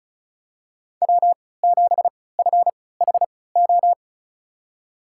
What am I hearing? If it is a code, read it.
Morse code, heard as a keyed tone, about 36 wpm, spaced as for 20 wpm, 700 Hz. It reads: W7FHO
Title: USA Call Signs - 5 Characters - 3x Character Spacing - 35wpm